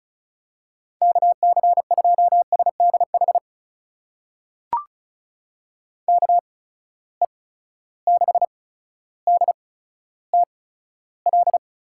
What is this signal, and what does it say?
Morse code, heard as a keyed tone, about 36 wpm, spaced as for 15 wpm, 700 Hz. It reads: KC2SDH KE6DTL
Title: USA Call Signs - 6 Characters - 8x Character Spacing - 35wpm